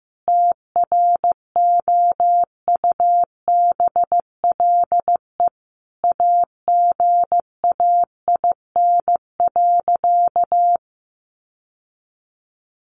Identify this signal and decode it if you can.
Morse code, heard as a keyed tone, about 15 wpm, 700 Hz. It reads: TROUBLE AGAIN.
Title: Sentences from Top 1200 Words 15wpm